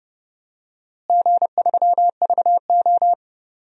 Morse code, heard as a keyed tone, about 30 wpm, 700 Hz. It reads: G3VO
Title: DXCC-100 Call Signs - 4 Characters - Repeated 3x - 30wpm